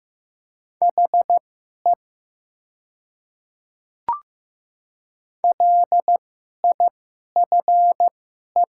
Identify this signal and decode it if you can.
Morse code, heard as a keyed tone, about 15 wpm, spaced as for 11 wpm, 700 Hz. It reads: HE LIFE
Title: 2x Character Spacing - Top 300 Words - 15wpm